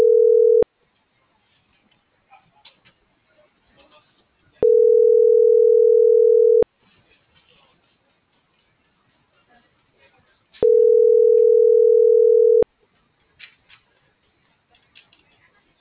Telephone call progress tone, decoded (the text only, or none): ringback tone